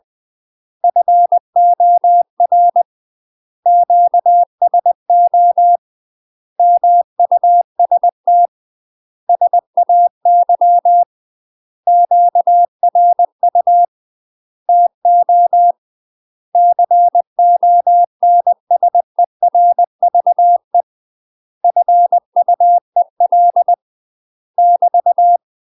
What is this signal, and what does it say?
Morse code, heard as a keyed tone, about 20 wpm, 700 Hz. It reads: FOR QSO MUST SAY QRU TO CONSERVE FUEL <BT>